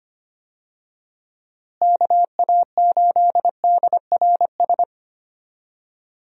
Morse code, heard as a keyed tone, about 25 wpm, 700 Hz. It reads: KA8DRS
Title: USA Call Signs - 6 Characters - Repeated 2x - 25wpm